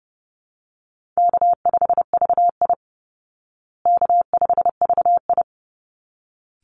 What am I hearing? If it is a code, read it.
Morse code, heard as a keyed tone, about 30 wpm, 700 Hz. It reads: K5VI K5VI
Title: Easy Call Signs Repeated 2x - 30wpm